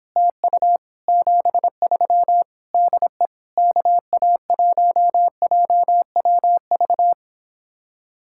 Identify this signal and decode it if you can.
Morse code, heard as a keyed tone, about 31 wpm, 700 Hz. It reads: TU 73 DE KA1JWV